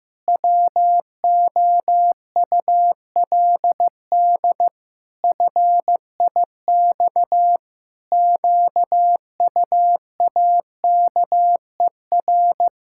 Morse code, heard as a keyed tone, about 16 wpm, 700 Hz. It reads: WOULD FIX QUAKER
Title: Pangram Challenge v2 - 15wpm